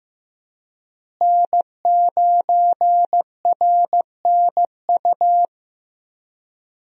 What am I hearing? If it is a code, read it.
Morse code, heard as a keyed tone, about 15 wpm, 700 Hz. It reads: N9RNU
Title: USA Call Signs - 5 Characters - Repeated 2x - 15wpm